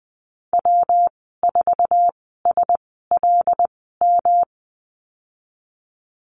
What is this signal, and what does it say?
Morse code, heard as a keyed tone, about 20 wpm, spaced as for 15 wpm, 700 Hz. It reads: W4SLM
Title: 154 Lesson 25g - Letter p - Call Signs 20wpm